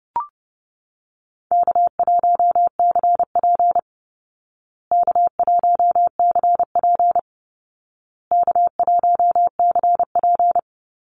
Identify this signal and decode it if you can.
Morse code, heard as a keyed tone, about 28 wpm, 700 Hz. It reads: K1CP K1CP K1CP
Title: Call Signs Repeated 2x - 30wpm